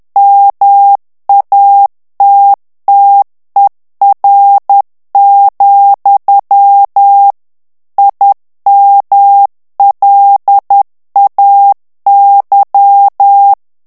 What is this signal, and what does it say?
Morse code, heard as a keyed tone, about 10 wpm, 800 Hz. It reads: MATTER, IMLAY
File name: ANGLAIS texte clair 3 vitesse 10 mots minute